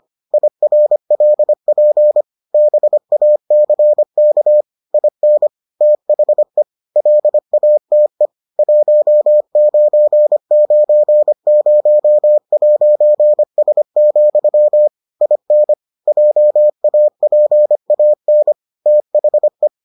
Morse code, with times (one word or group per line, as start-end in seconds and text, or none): 0.3-2.2 IRLP
2.5-4.6 BACK
4.9-5.5 IN
5.8-6.6 THE
7.0-8.3 LATE
8.6-14.9 1990'S,
15.2-15.7 IN
16.1-18.5 JAPAN
18.9-19.7 THE